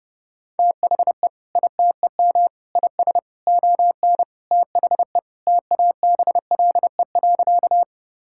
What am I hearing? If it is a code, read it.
Morse code, heard as a keyed tone, about 30 wpm, 700 Hz. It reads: THE ITEM IS ON THE TABLE.
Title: Sentences from Top 1000 Words - Review Part 6 - 30wpm